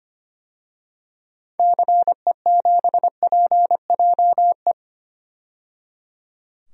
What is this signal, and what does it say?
Morse code, heard as a keyed tone, about 25 wpm, 700 Hz. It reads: CE7PJE
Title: DXCC-100 Call Signs - 6 Characters - 25wpm